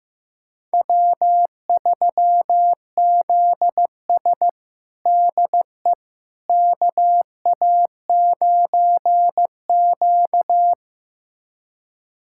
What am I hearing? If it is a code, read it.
Morse code, heard as a keyed tone, about 15 wpm, 700 Hz. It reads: W3ZS DE KA9Q
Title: USA Callsign DE Callsign - 4 Characters - 15wpm